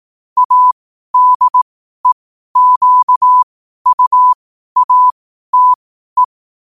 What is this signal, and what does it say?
Morse code, heard as a keyed tone, about 18 wpm, spaced as for 12 wpm, 1000 Hz. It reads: ADEQUATE
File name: qotdfiles / 12wpm